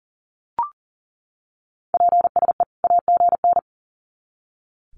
Morse code, heard as a keyed tone, about 40 wpm, 700 Hz. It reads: PSE AGN
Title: Common QSO Phrases 40wpm